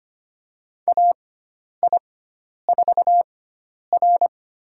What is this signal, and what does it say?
Morse code, heard as a keyed tone, about 25 wpm, spaced as for 10 wpm, 700 Hz. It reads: AI4R
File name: Lesson 16g - Letter c - Call Signs 25wpm